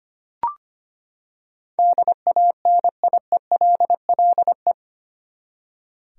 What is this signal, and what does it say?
Morse code, heard as a keyed tone, about 25 wpm, 700 Hz. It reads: DANIELLE
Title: USA Common Names - 25wpm